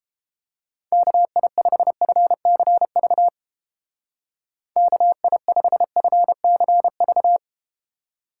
Morse code, heard as a keyed tone, about 30 wpm, 700 Hz. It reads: KI5FCV KI5FCV